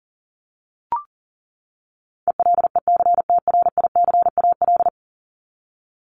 Morse code, heard as a keyed tone, about 40 wpm, 700 Hz. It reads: ELECTRICAL
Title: Top 751-1000 Words 40wpm